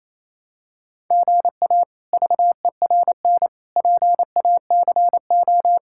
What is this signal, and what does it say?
Morse code, heard as a keyed tone, about 28 wpm, 700 Hz. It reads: GA VERN PACO
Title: K1USN SST QSO Practice - 28wpm